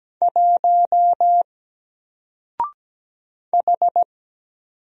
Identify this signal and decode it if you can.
Morse code, heard as a keyed tone, about 17 wpm, 700 Hz. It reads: 1 H
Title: Single Character (Letter:Number:Punctuation:Prosign) Alt 17wpm